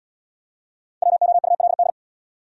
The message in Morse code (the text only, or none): WOULD